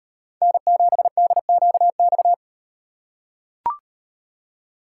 Morse code, heard as a keyed tone, about 37 wpm, 700 Hz. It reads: N7DQX